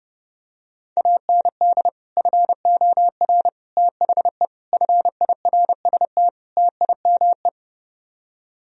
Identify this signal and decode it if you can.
Morse code, heard as a keyed tone, about 30 wpm, 700 Hz. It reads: AND FOR THE FIRST TIME